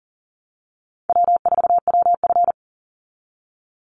Morse code, heard as a keyed tone, about 40 wpm, 700 Hz. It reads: W4WF
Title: Easy Call Signs Repeated 2x - 40wpm